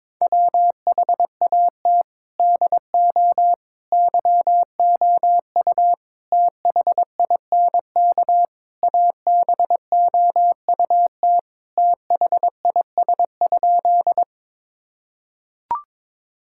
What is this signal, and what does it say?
Morse code, heard as a keyed tone, about 22 wpm, 700 Hz. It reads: WHAT DO YOU THINK ABOUT THIS?
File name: practice / 22wpm / Sentences with 4-Word Phrases - 22wpm